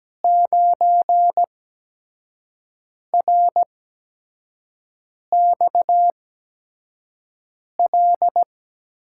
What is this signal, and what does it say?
Morse code, heard as a keyed tone, about 17 wpm, spaced as for 5 wpm, 700 Hz. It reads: 9RXL
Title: USA Call Signs - 6 Characters - 8x Character Spacing - 17wpm